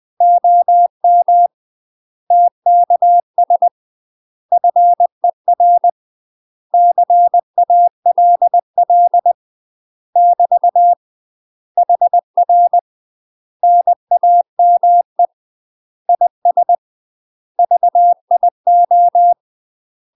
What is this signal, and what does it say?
Morse code, heard as a keyed tone, about 20 wpm, 700 Hz. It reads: OM TKS FER CALL <BT> HR NAME IS VIO
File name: UK MASTER QSOs TRAINING / 20WPM / UK MASTER QSOs 0047